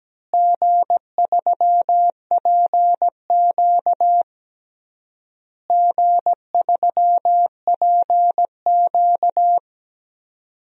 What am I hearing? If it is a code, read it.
Morse code, heard as a keyed tone, about 17 wpm, 700 Hz. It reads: G3PQ G3PQ
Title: DXCC-100 Call Signs - 4 Characters - Repeated 3x - 17wpm